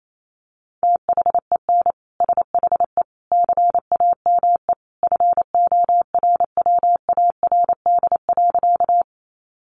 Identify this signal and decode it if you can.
Morse code, heard as a keyed tone, about 28 wpm, 700 Hz. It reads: THEN SHE CAME FORWARD.
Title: Sentences from Top 1300 Words 28wpm